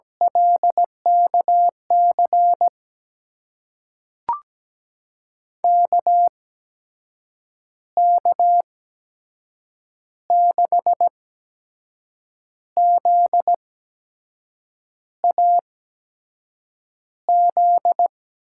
Morse code, heard as a keyed tone, about 17 wpm, spaced as for 6 wpm, 700 Hz. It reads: LKC KK6ZAZ
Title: USA Call Signs - 6 Characters - 8x Character Spacing - 17wpm